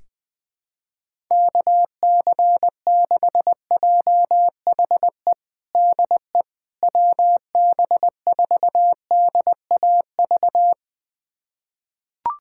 Morse code, heard as a keyed tone, about 20 wpm, 700 Hz. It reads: KC6JHE DE WB4DAV
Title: USA Callsign DE Callsign 20wpm